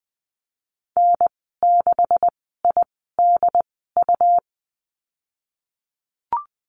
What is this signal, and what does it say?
Morse code, heard as a keyed tone, about 20 wpm, spaced as for 15 wpm, 700 Hz. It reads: N6IDU
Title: 167 Lesson 27g - Number 7 - Call Signs 20wpm